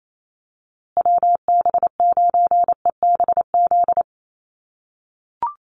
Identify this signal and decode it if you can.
Morse code, heard as a keyed tone, about 28 wpm, 690 Hz. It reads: WB9EBZ
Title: USA Call Signs Repeated 3x 28wpm